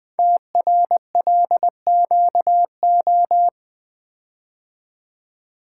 Morse code, heard as a keyed tone, about 20 wpm, 700 Hz. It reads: TRLQO
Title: Code Groups - 5 Letter:Numbers - 20wpm